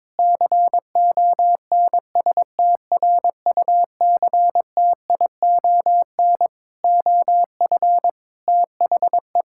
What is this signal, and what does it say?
Morse code, heard as a keyed tone, about 22 wpm, 700 Hz. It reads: CONSTRUCTION OF THE
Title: Sentences from Top 1600 Words 22wpm